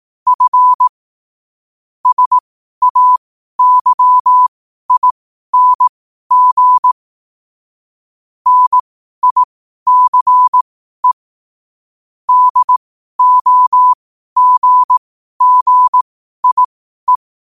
Morse code, heard as a keyed tone, about 18 wpm, spaced as for 12 wpm, 1000 Hz. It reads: F SAYING NICE DOGGIE
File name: qotdfiles / 12wpm